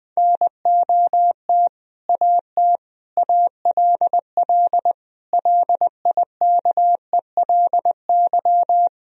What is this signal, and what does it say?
Morse code, heard as a keyed tone, about 20 wpm, 700 Hz. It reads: NOT AT ALL LIKELY
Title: Sentences from Top 1000 Words - Review Part 4 - 20wpm